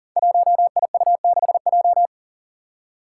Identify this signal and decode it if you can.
Morse code, heard as a keyed tone, about 38 wpm, 700 Hz. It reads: 1IU6J
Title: Five Challenging Letter-Number Combinations 40wpm